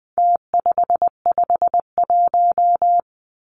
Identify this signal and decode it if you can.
Morse code, heard as a keyed tone, about 20 wpm, 700 Hz. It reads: T551
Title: Numbers - Contest Style - 4-Digit 20wpm